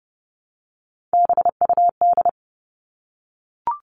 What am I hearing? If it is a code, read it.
Morse code, heard as a keyed tone, about 30 wpm, 700 Hz. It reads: BUD
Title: All Cut Numbers - 3-Digit 30wpm